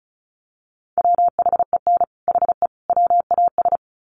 Morse code, heard as a keyed tone, about 34 wpm, 700 Hz. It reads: WHEN HE WAS
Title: Sets of 3 Words 35wpm